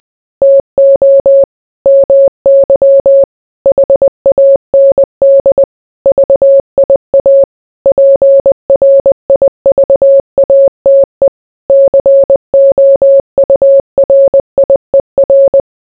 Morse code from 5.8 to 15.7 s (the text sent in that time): VIA PRIVATE COURIER